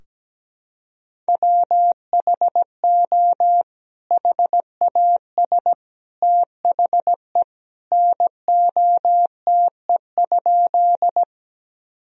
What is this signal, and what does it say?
Morse code, heard as a keyed tone, about 17 wpm, 700 Hz. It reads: WHO HAS THE NOTE?
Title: Sentences from Top 1000 Words - Review Part 2 - 17wpm